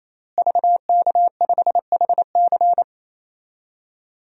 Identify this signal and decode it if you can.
Morse code, heard as a keyed tone, about 28 wpm, 700 Hz. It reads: VK5HC
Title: DXCC-100 Call Signs - 5 Characters - Repeated 3x - 28wpm